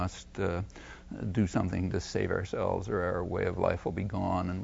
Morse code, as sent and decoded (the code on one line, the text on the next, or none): none